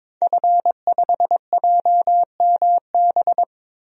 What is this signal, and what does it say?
Morse code, heard as a keyed tone, about 22 wpm, 700 Hz. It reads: F5JMB
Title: DXCC-100 Call Signs - 5 Characters - Repeated 2x - 22wpm